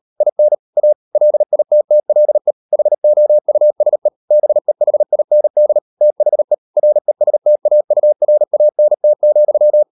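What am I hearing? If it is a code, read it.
Morse code, heard as a keyed tone, about 38 wpm, 600 Hz. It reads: IN A LITTLE HOUSE BEHIND THE RESTAURANT,
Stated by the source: Morse code practice at 38 WPM, sending Ken Follett's novel Never in English, from CW med Gnister.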